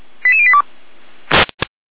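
Five-tone selective call (ZVEI), heard as quick runs of five tones, tones of about 70 ms. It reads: 88082
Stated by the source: ZVEI five-tone selective call received on SDR.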